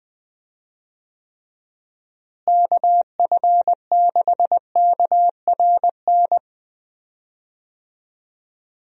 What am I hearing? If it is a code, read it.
Morse code, heard as a keyed tone, about 20 wpm, 700 Hz. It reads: KF6KRN
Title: USA Call Signs - 6 Characters - Repeated 2x - 20wpm